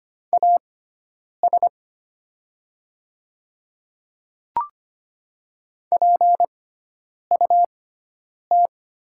Morse code, heard as a keyed tone, about 25 wpm, spaced as for 9 wpm, 700 Hz. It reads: AS PUT
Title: 6x Character Spacing - Top 200 Words - 25wpm